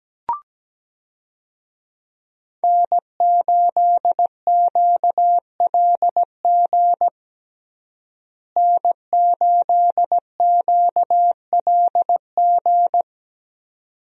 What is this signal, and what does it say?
Morse code, heard as a keyed tone, about 17 wpm, 700 Hz. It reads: N8QLG N8QLG
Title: USA Call Signs - 5 Characters - Repeated 2x - 17wpm